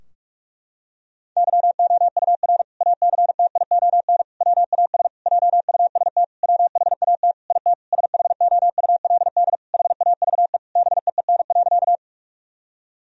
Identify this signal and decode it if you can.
Morse code, heard as a keyed tone, about 46 wpm, 700 Hz. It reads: YOUR ACTION WAS JUST WHAT IT SHOULD HAVE BEEN.
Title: Sentences from Top 1000 Words - Review Part 4 - 45wpm